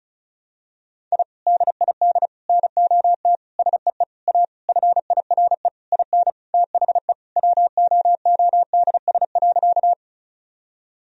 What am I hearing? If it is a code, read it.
Morse code, heard as a keyed tone, about 36 wpm, 700 Hz. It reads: I DID NOT SEE A FIRE IN THE WOODS.
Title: Sentences from Top 1000 Words - Review Part 2 - 35wpm